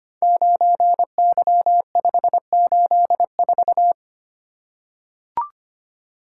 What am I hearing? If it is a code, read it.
Morse code, heard as a keyed tone, about 25 wpm, 700 Hz. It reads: 9Y584
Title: Code Groups - 5 Letter:Numbers - 25wpm